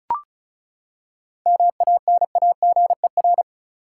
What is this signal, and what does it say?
Morse code, heard as a keyed tone, about 36 wpm, 700 Hz. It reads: MANAGER
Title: Seven Letter Words - 35wpm